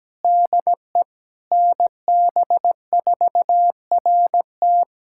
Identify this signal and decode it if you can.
Morse code, heard as a keyed tone, about 17 wpm, 700 Hz. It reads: DE NB4RT
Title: USA Callsign DE Callsign 17wpm